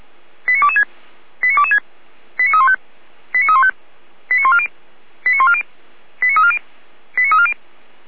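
Five-tone selective call (ZVEI), heard as quick runs of five tones, tones of about 70 ms. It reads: 89226 89226 89315 89315 89140 89140 89340 89340
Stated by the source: ZVEI five-tone selective call received via SDR.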